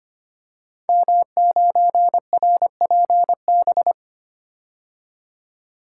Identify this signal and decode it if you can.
Morse code, heard as a keyed tone, about 25 wpm, 700 Hz. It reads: M9RPB
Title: Code Groups - 5 Letter:Numbers - 25wpm